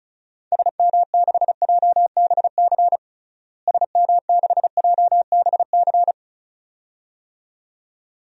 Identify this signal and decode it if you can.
Morse code, heard as a keyed tone, about 34 wpm, 700 Hz. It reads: SM6JBC SM6JBC
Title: DXCC-100 Call Signs - 6 Characters - Repeated 2x - 35wpm